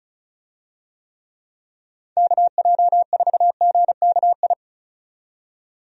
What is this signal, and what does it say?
Morse code, heard as a keyed tone, about 36 wpm, 700 Hz. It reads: KJ4GKI
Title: USA Call Signs - 6 Characters - 35wpm